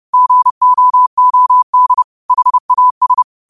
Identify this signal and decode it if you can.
Morse code, heard as a keyed tone, about 30 wpm, 1000 Hz. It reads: GOOD HAS
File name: qotdfiles / 30wpm